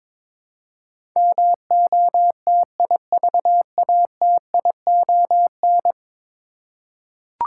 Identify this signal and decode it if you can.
Morse code, heard as a keyed tone, about 22 wpm, 700 Hz. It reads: MOTIVATION
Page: morse-code-ninja-practice-22wpm